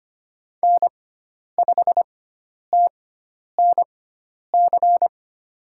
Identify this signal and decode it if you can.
Morse code, heard as a keyed tone, about 25 wpm, spaced as for 10 wpm, 700 Hz. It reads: N5TNC
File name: Lesson 16g - Letter c - Call Signs 25wpm